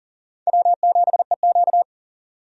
Morse code, heard as a keyed tone, about 39 wpm, 690 Hz. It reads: W7EQ
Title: USA Call Signs - 4 Characters - 40wpm